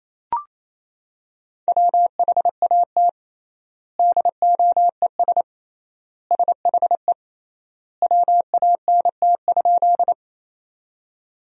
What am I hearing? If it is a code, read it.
Morse code, heard as a keyed tone, about 27 wpm, 700 Hz. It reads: WHAT DOES SHE WANT?
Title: 3x Word-Spacing - Sentences from Top 200 Words - 28wpm